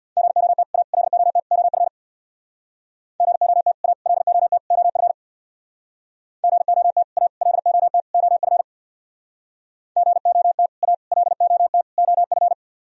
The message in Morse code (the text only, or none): GOT A LOT OF GOT A LOT OF GOT A LOT OF GOT A LOT OF